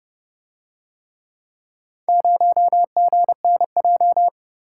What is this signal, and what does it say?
Morse code, heard as a keyed tone, about 30 wpm, 700 Hz. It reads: 0GNJ